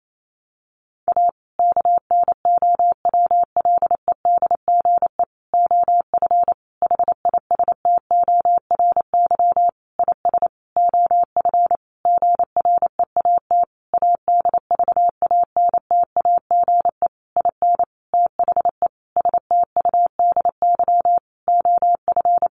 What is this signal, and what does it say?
Morse code, heard as a keyed tone, about 28 wpm, 690 Hz. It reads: A KNOWLEDGE OF HISTORY IS OF GREAT ADVANTAGE IN THE STUDY OF
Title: Sentences from Top 900 Words 28wpm